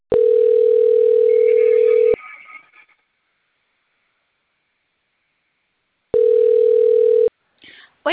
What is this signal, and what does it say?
Ringback tone (North America), 440 + 480 Hz, with speech.
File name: out- - - - - .3566